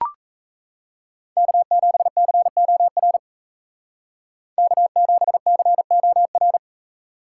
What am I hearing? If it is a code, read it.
Morse code, heard as a keyed tone, about 36 wpm, 700 Hz. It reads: K7COR K7COR